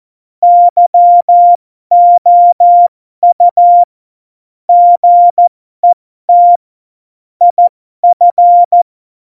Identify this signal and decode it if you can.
Morse code, heard as a keyed tone, about 14 wpm, spaced as for 12 wpm, 700 Hz. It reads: YOU GET IF
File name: Morsekurs / Witze